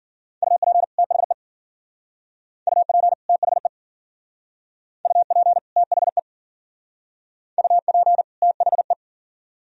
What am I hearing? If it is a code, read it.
Morse code, heard as a keyed tone, about 50 wpm, 700 Hz. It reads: UP THE UP THE UP THE UP THE
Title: Speed-Racing - Sets of 2 Words 40wpm